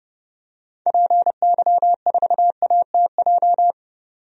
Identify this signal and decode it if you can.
Morse code, heard as a keyed tone, about 31 wpm, 700 Hz. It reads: PY4ATJ
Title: DXCC-100 Call Signs - 6 Characters - 30wpm